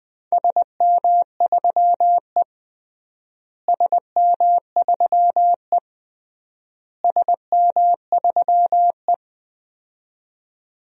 Morse code, heard as a keyed tone, about 20 wpm, 700 Hz. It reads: SM3E SM3E SM3E
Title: DXCC-100 Call Signs - 4 Characters - Repeated 3x - 20wpm